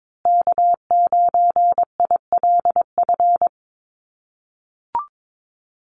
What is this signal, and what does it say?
Morse code, heard as a keyed tone, about 22 wpm, 700 Hz. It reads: K9ILF
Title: USA Call Signs Repeated 3x 22wpm